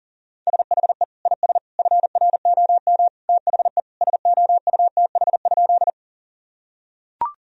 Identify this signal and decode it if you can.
Morse code, heard as a keyed tone, about 37 wpm, 700 Hz. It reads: SHE IS FROM THE SOUTH?
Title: Sentences from Top 1000 Words - Review Part 2 - 40wpm